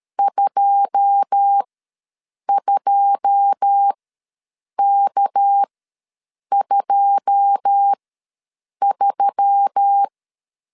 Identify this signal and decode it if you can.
Morse code, heard as a keyed tone, about 13 wpm, spaced as for 7 wpm, 800 Hz. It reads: 22K23
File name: LEZIONI / 9 2 ws 12wpm 3406